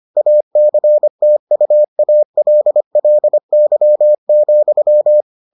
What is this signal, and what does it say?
Morse code, heard as a keyed tone, about 25 wpm, 600 Hz. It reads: ACTUALLY,